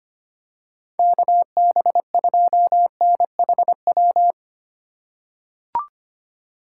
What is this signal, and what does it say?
Morse code, heard as a keyed tone, about 25 wpm, 700 Hz. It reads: KB2NHW
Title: USA Call Signs - 6 Characters - 4x Character Spacing - 25wpm